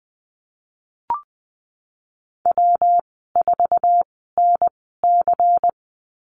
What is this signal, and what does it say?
Morse code, heard as a keyed tone, about 20 wpm, spaced as for 15 wpm, 700 Hz. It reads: W4NC
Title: 154 Lesson 25g - Letter p - Call Signs 20wpm